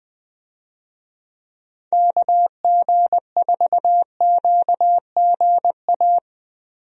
Morse code, heard as a keyed tone, about 20 wpm, 700 Hz. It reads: KG4QGA
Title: USA Call Signs - 6 Characters - Repeated 3x - 20wpm